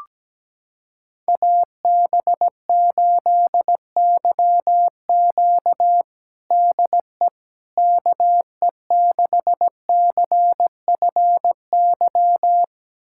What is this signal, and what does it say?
Morse code, heard as a keyed tone, about 17 wpm, 700 Hz. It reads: AB8YQ DE KE6CFY
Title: USA Callsign DE Callsign 17wpm